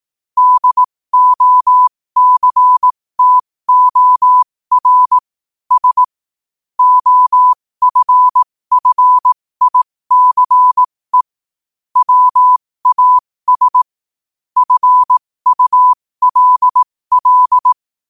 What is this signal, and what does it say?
Morse code, heard as a keyed tone, about 18 wpm, spaced as for 15 wpm, 1000 Hz. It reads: DOCTOR S OFFICE WAS FULL
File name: qotdfiles / 15wpm